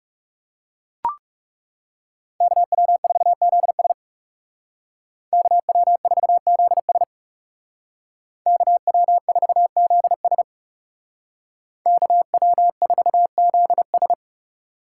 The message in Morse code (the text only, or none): KW4ZS KW4ZS KW4ZS KW4ZS